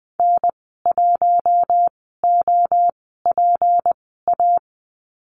Morse code, heard as a keyed tone, about 20 wpm, spaced as for 14 wpm, 700 Hz. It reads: N1OPA
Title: 161 Lesson 26g - Letter g - Call Signs 20wpm